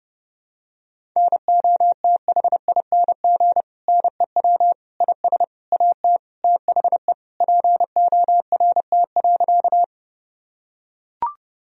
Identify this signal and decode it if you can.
Morse code, heard as a keyed tone, about 32 wpm, 700 Hz. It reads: NOTHING NEW IS AT THE PORT.
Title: Sentences from Top 1000 Words - Review Part 1 - 30wpm